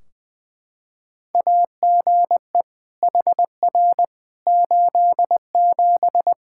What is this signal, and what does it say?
Morse code, heard as a keyed tone, about 21 wpm, 700 Hz. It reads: AGE HR 87
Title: Practice QSO - Exchange 3 - More Info - 20wpm